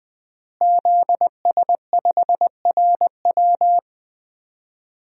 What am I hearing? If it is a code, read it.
Morse code, heard as a keyed tone, about 20 wpm, 700 Hz. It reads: ZS5RW
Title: DXCC-100 Call Signs - 5 Characters - Repeated 3x - 20wpm